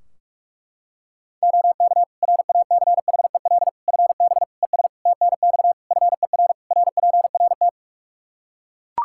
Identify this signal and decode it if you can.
Morse code, heard as a keyed tone, about 41 wpm, 700 Hz. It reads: OK RACHEL FB ES TNX FER RPRT
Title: Practice QSO - Exchange 2 - Info - 45wpm